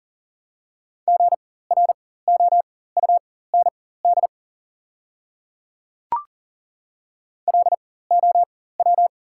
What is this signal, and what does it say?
Morse code, heard as a keyed tone, about 40 wpm, spaced as for 19 wpm, 700 Hz. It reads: GROUND LOW
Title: 4x Character Spacing - Top 500 Words - 40wpm